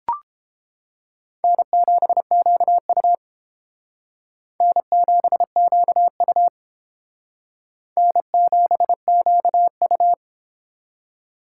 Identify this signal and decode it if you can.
Morse code, heard as a keyed tone, about 29 wpm, 700 Hz. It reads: N7QU N7QU N7QU